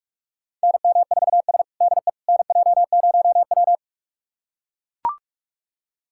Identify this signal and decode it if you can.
Morse code, heard as a keyed tone, about 44 wpm, 700 Hz. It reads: NM4S DE NJ0W
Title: USA Callsign DE Callsign - 4 Characters - 45wpm